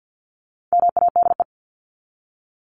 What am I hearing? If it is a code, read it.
Morse code, heard as a keyed tone, about 50 wpm, 700 Hz. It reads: MADE
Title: Top 500 Words 50wpm